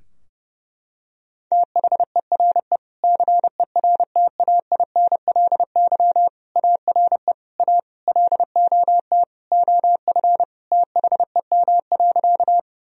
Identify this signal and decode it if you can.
Morse code, heard as a keyed tone, about 30 wpm, 700 Hz. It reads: THERE CERTAINLY ARE A LOT OF THEM.
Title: Sentences from Top 1000 Words - Review Part 3 - 30wpm